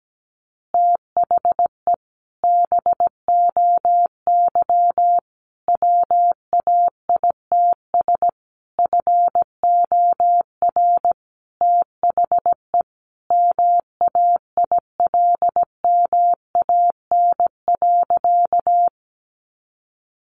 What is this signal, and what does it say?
Morse code, heard as a keyed tone, about 17 wpm, 700 Hz. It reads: THE BOY WAITS FOR THE MAILMAN.
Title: Additional Easy Phrases 17wpm